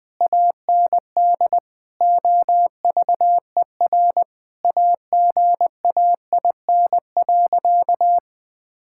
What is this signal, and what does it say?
Morse code, heard as a keyed tone, about 20 wpm, 700 Hz. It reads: AND OVER AGAIN.
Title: Sentences with 6-Word Phrases - 20wpm